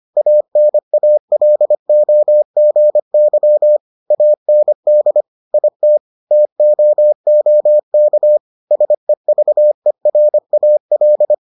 Morse code, heard as a keyed tone, about 25 wpm, 600 Hz. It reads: ANALOGY AND IT TOOK SEVERAL